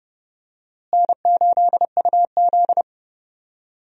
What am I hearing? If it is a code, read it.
Morse code, heard as a keyed tone, about 30 wpm, 700 Hz. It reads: N8UZ
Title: USA Call Signs - 4 Characters - Repeated 3x - 30wpm